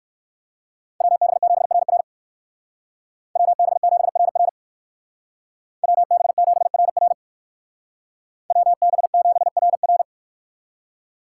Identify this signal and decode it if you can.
Morse code, heard as a keyed tone, about 56 wpm, 700 Hz. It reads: WB7RR WB7RR WB7RR WB7RR
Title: Speed Racing - USA 2x2 Call Signs 45wpm